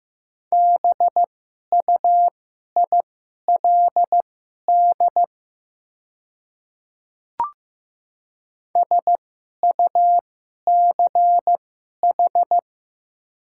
Morse code, heard as a keyed tone, about 15 wpm, spaced as for 11 wpm, 700 Hz. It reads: BUILD SUCH
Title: 2x Character Spacing - Top 500 Words - 15wpm